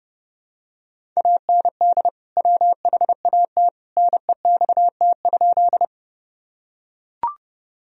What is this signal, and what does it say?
Morse code, heard as a keyed tone, about 30 wpm, 700 Hz. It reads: AND WHAT NEXT?
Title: Sentences from Top 1000 Words - Review Part 1 - 30wpm